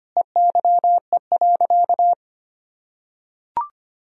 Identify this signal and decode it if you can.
Morse code, heard as a keyed tone, about 25 wpm, 700 Hz. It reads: EYE.
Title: Sentences from Top 1000 Words - Review Part 2 - 25wpm